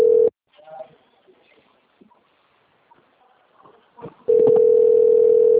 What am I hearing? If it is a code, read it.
Ringback tone (North America), 440 + 480 Hz.
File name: out- - - - - .11154